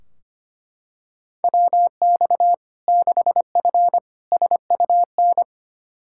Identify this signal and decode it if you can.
Morse code, heard as a keyed tone, about 25 wpm, 700 Hz. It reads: WX 6F SUN
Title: Practice QSO - Exchange 2 - Info - 25wpm